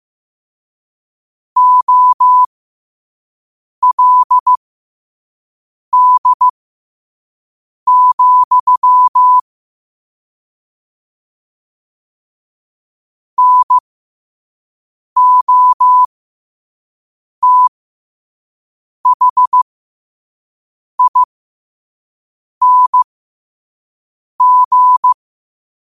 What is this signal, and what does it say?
Morse code, heard as a keyed tone, about 15 wpm, spaced as for 5 wpm, 1000 Hz. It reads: OLD, NOTHING